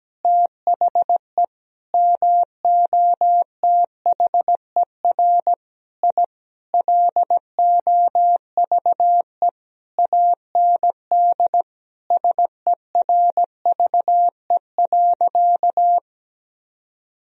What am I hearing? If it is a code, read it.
Morse code, heard as a keyed tone, about 17 wpm, 700 Hz. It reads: THE MOTHER I LOVE AND SERVE.
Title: Sentences from Top 1000 Words - Review Part 2 - 17wpm